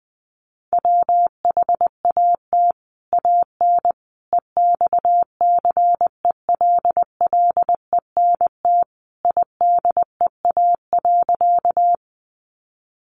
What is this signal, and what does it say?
Morse code, heard as a keyed tone, about 20 wpm, 700 Hz. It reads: WHAT AN EXCELLENT IDEA.
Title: Sentences from Top 1000 Words 20wpm